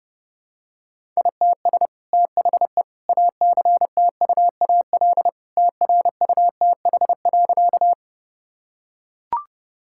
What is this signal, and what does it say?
Morse code, heard as a keyed tone, about 30 wpm, 700 Hz. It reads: ITS THE ACTUAL TRUTH.
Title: Sentences from Top 1000 Words - Review Part 5 - 30wpm